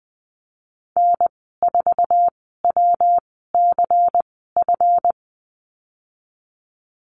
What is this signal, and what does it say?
Morse code, heard as a keyed tone, about 20 wpm, spaced as for 15 wpm, 700 Hz. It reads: N4WCF
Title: 167 Lesson 27g - Number 7 - Call Signs 20wpm